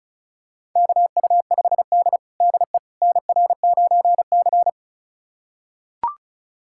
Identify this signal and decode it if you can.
Morse code, heard as a keyed tone, about 36 wpm, 700 Hz. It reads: KU5D DE NR9C
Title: USA Callsign DE Callsign - 4 Characters - 35wpm